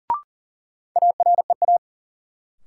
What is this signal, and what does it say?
Morse code, heard as a keyed tone, about 40 wpm, 700 Hz. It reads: AREA
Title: Sentences from Top 1000 Words - Review Part 2 - New Word Drill - 40wpm